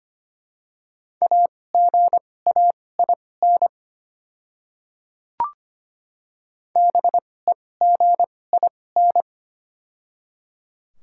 Morse code, heard as a keyed tone, about 25 wpm, spaced as for 18 wpm, 700 Hz. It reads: AGAIN BEGIN
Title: 2x Character Spacing - Top 500 Words - 25wpm